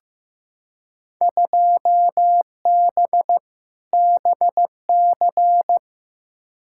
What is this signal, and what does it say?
Morse code, heard as a keyed tone, about 16 wpm, 700 Hz. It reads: 2B BC
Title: Field Day Exchanges Encore - 15wpm